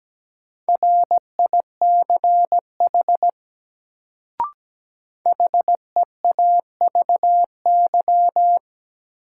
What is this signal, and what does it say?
Morse code, heard as a keyed tone, about 17 wpm, 700 Hz. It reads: RICH HEAVY
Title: Sentences from Top 1100 Words - New Word Drill Alt - 17wpm